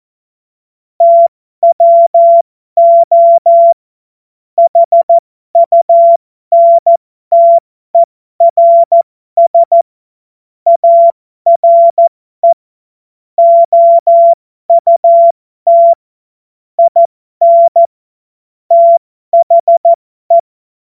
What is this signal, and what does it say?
Morse code, heard as a keyed tone, about 14 wpm, spaced as for 12 wpm, 680 Hz. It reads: TWO HUNTERS ARE OUT IN THE
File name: Morsekurs / Witze